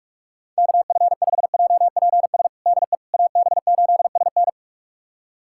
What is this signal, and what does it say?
Morse code, heard as a keyed tone, about 46 wpm, 700 Hz. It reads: KF5JPS DE AB8SN
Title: USA Callsign DE Callsign 45wpm